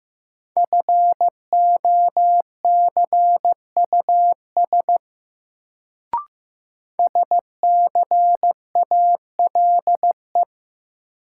Morse code, heard as a keyed tone, about 15 wpm, 700 Hz. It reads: FOCUS SCALE
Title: Sentences from Top 1000 Words - Review Part 4 - New Word Drill - 15wpm